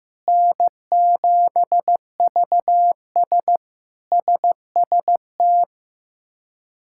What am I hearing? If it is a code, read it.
Morse code, heard as a keyed tone, about 16 wpm, 700 Hz. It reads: N7VS SST
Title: K1USN SST QSO Practice - 15wpm